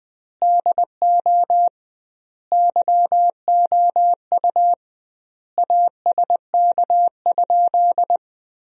Morse code, heard as a keyed tone, about 20 wpm, 700 Hz. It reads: DO YOU ASK?
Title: 2x Word-Spacing - Sentences from Top 200 Words - 20wpm